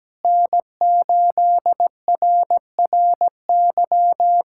Morse code, heard as a keyed tone, about 17 wpm, 700 Hz. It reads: N8RRY
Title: USA Callsign DE Callsign - 5 Characters - 17wpm